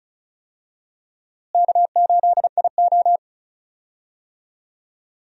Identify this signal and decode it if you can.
Morse code, heard as a keyed tone, about 34 wpm, 700 Hz. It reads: K8IO
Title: USA Call Signs - 4 Characters - Repeated 2x - 35wpm